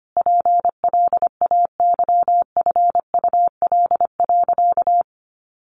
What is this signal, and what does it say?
Morse code, heard as a keyed tone, about 25 wpm, 700 Hz. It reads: PLAYFUL.
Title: Easy Phrases 25wpm